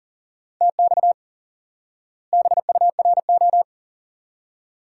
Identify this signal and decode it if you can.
Morse code, heard as a keyed tone, about 40 wpm, 700 Hz. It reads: TX BURO
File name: IQE Course 3 - Lesson 89 - YAGI - Two Elements - 40wpm